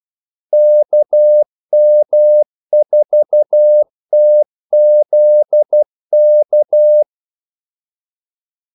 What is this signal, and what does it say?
Morse code, heard as a keyed tone, about 12 wpm, 600 Hz. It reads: KM4TZK
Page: 2 ekstra ordmellemrum mellem kaldesignalerne